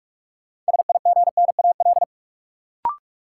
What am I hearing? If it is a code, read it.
Morse code, heard as a keyed tone, about 46 wpm, 700 Hz. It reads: SIGNAL
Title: Sentences from Top 1400 Words - New Word Drill Alt - 45wpm